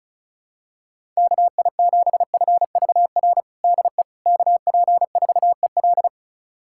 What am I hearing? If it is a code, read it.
Morse code, heard as a keyed tone, about 36 wpm, 700 Hz. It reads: KI7FVR DE KP4EL
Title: USA Callsign DE Callsign 35wpm